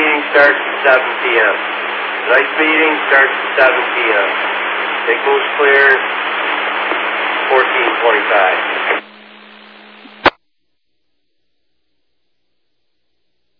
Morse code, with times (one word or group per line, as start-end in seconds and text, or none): none